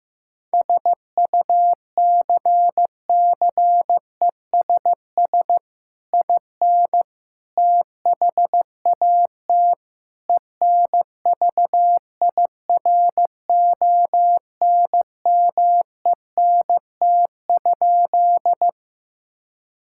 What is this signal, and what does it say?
Morse code, heard as a keyed tone, about 15 wpm, 700 Hz. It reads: SUCCESS IN THAT ENVIRONMENT?
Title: Sentences from Top 1000 Words - Review Part 5 - 15wpm